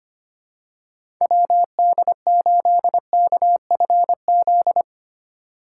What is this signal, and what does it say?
Morse code, heard as a keyed tone, about 25 wpm, 700 Hz. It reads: WD8KFZ
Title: USA Call Signs - 6 Characters - 2x Character Spacing - 25wpm